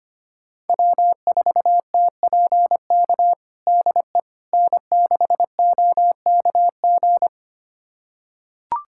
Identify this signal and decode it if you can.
Morse code, heard as a keyed tone, about 25 wpm, 700 Hz. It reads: W4TPK DE N6OKG
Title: USA Callsign DE Callsign - 5 Characters - 25wpm